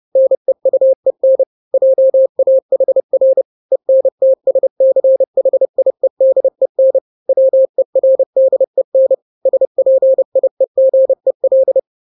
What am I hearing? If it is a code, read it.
Morse code, heard as a keyed tone, about 30 wpm, 530 Hz. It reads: NEUEN JAHR ENTSCHIEDEN WERDEN SPIEGEL
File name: 29wpm